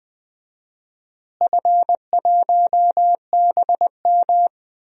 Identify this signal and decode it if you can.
Morse code, heard as a keyed tone, about 20 wpm, 700 Hz. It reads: F1BM
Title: DXCC-100 Call Signs - 4 Characters - Repeated 2x - 20wpm